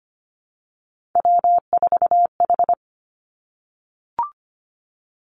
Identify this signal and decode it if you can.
Morse code, heard as a keyed tone, about 25 wpm, 700 Hz. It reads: W4H
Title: Three Challenging Letter-Number Combinations 25wpm